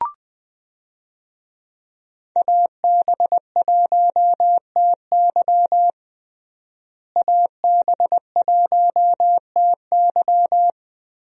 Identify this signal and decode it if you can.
Morse code, heard as a keyed tone, about 20 wpm, 700 Hz. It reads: AB1TY AB1TY
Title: USA Call Signs - 5 Characters - Repeated 2x - 20wpm